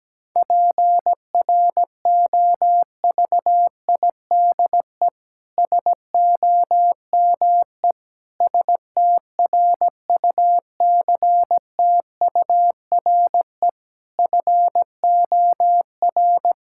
Morse code, heard as a keyed tone, about 17 wpm, 700 Hz. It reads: PROVIDE SOME STRUCTURE FOR
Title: Sentences from Top 1000 Words - Review Part 6 - 17wpm